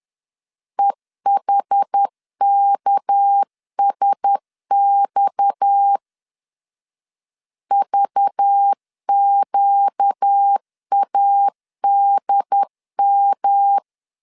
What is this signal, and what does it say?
Morse code, heard as a keyed tone, about 10 wpm, 800 Hz. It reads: EHKSX VQADM
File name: CW / DET / 1 6 1500C S B 10wpm 4137